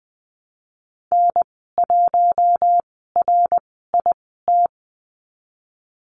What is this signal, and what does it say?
Morse code, heard as a keyed tone, about 20 wpm, spaced as for 15 wpm, 700 Hz. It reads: N1RIT
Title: 062 Lesson 11g - Letter h - Call Signs 20wpm